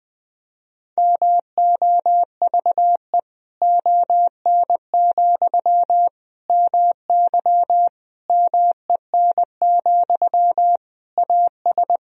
Morse code, heard as a keyed tone, about 21 wpm, 700 Hz. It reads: MOVE ON, MY MEN, AS
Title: Sentences from Top 500 Words and Abbreviations 20wpm